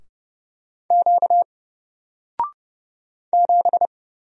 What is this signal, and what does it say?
Morse code, heard as a keyed tone, about 30 wpm, 700 Hz. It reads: Q 7
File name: Single-Letter-Number-30wpm